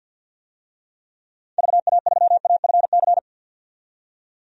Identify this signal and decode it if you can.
Morse code, heard as a keyed tone, about 51 wpm, 700 Hz. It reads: VA3AFC
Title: DXCC-100 Call Signs - 6 Characters - 50wpm